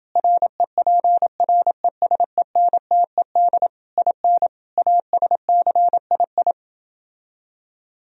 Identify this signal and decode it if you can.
Morse code, heard as a keyed tone, about 29 wpm, 700 Hz. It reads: REPRESENTED IN ASCII